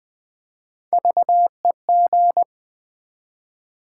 Morse code, heard as a keyed tone, about 20 wpm, 700 Hz. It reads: VEG
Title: Build A Word - 3 Letters At A Time - 9-Letter Words - 20wpm